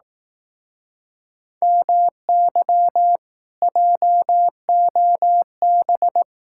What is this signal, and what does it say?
Morse code, heard as a keyed tone, about 18 wpm, 700 Hz. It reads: MY JOB
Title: Speed-Racing Common QSO Phrases Encore - 15wpm